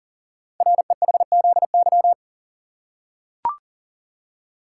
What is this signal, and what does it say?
Morse code, heard as a keyed tone, about 42 wpm, 700 Hz. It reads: REHZY
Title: Code Groups - 5 Letters - 40wpm